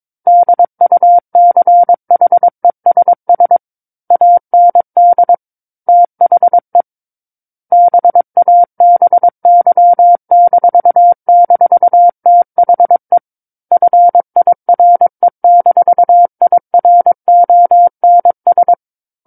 Morse code, heard as a keyed tone, about 22 wpm, 700 Hz. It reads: DUCHESS AND THE BABY--THE FIRE-IRONS
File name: 06-Chapter-06 22 WPM